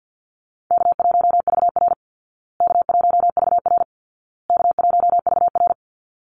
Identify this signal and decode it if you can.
Morse code, heard as a keyed tone, about 50 wpm, 700 Hz. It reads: K1VR K1VR K1VR
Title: Easy Call Signs Repeated 3x - 50wpm